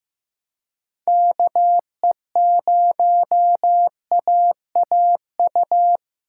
Morse code, heard as a keyed tone, about 15 wpm, 700 Hz. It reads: KE0AAU
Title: USA Call Signs - 4x Character Spacing - 15wpm